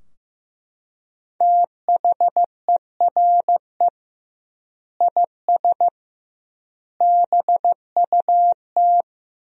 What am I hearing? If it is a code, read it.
Morse code, heard as a keyed tone, about 15 wpm, 700 Hz. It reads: THERE IS BUT
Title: 2x Word-Spacing - Sentences from Top 100 Words - 15wpm